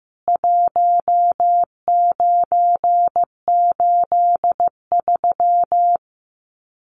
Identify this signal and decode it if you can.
Morse code, heard as a keyed tone, about 14 wpm, 700 Hz. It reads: 1983
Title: Common QSO Phrases 15wpm